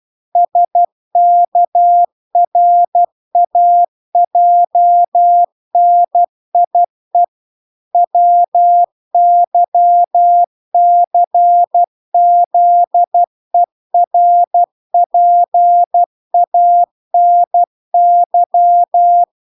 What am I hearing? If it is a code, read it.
Morse code, heard as a keyed tone, about 12 wpm, 700 Hz. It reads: SKRAJNIE WYCZERPANY